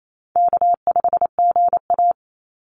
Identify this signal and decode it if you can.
Morse code, heard as a keyed tone, about 28 wpm, 700 Hz. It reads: K5GA
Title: Easy Call Signs Repeated 2x - 28wpm